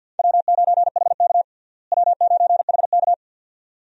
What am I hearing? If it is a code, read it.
Morse code, heard as a keyed tone, about 51 wpm, 700 Hz. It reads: W9HK W9HK
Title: USA Call Signs - 4 Characters - Repeated 2x - 50wpm